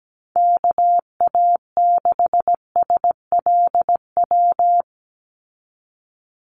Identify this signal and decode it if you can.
Morse code, heard as a keyed tone, about 17 wpm, 700 Hz. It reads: KA6SLW
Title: USA Call Signs 17wpm